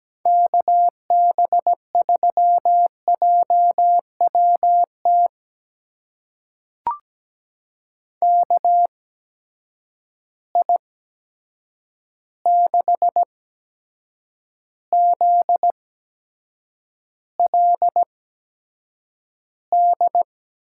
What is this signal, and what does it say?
Morse code, heard as a keyed tone, about 17 wpm, spaced as for 7 wpm, 700 Hz. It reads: KB3JWT KI6ZLD
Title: USA Call Signs - 6 Characters - 8x Character Spacing - 17wpm